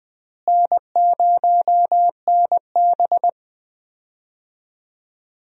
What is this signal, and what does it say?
Morse code, heard as a keyed tone, about 21 wpm, 700 Hz. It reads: N0NB
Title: USA Call Signs - 4 Characters - Repeated 2x - 20wpm